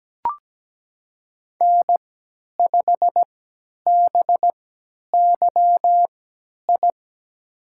Morse code, heard as a keyed tone, about 17 wpm, spaced as for 10 wpm, 700 Hz. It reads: N5BYI
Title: USA Call Signs - 3x Character Spacing - 17wpm